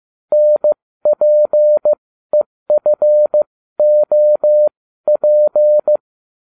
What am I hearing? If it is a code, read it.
Morse code, heard as a keyed tone, about 15 wpm, spaced as for 13 wpm, 600 Hz. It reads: NPEFOP